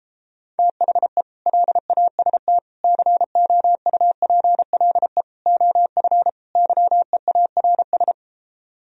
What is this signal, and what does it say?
Morse code, heard as a keyed tone, about 33 wpm, 700 Hz. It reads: THE LAST COUPLE OF YEARS